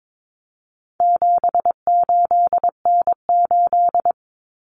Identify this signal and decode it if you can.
Morse code, heard as a keyed tone, about 22 wpm, 700 Hz. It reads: 78N8
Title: Common Cut Numbers - 4-Digit 22wpm